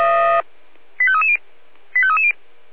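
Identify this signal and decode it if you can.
Five-tone selective call (ZVEI), heard as quick runs of five tones, tones of about 70 ms. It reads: 85330 85330
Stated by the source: ZVEI five-tone selective call 85350 received by SDR.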